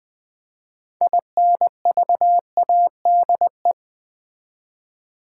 Morse code, heard as a keyed tone, about 20 wpm, 700 Hz. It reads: INVADE
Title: Build A Word - 3 Letters At A Time - 6-Letter Words - 20wpm